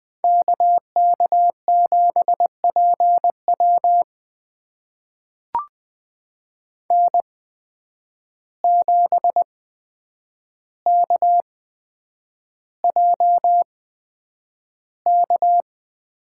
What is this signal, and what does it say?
Morse code, heard as a keyed tone, about 20 wpm, spaced as for 9 wpm, 700 Hz. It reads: KK7PW N7KJK
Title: USA Call Signs - 5 Characters - 8x Character Spacing - 20wpm